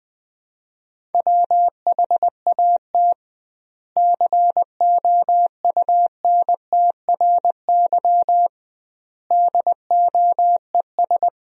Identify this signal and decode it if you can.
Morse code, heard as a keyed tone, about 21 wpm, 700 Hz. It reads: WHAT COUNTRY DOES
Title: 2x Word-Spacing - Sentences from Top 300 Words - 20wpm